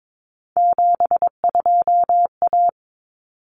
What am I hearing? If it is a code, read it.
Morse code, heard as a keyed tone, about 22 wpm, 700 Hz. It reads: 72A
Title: Common Cut Numbers - 3-Digit 22wpm